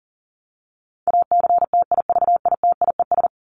Morse code, heard as a keyed tone, about 40 wpm, 700 Hz. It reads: ACTIVITIES
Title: ES:S Suffix - Top 100 Words with Base Words 40wpm